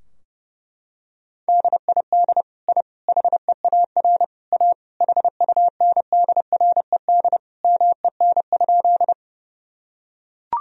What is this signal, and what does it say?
Morse code, heard as a keyed tone, about 30 wpm, 700 Hz. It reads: DID I HEAR A HUNDRED MEN?
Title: Sentences from Top 1000 Words - Review Part 2 - 30wpm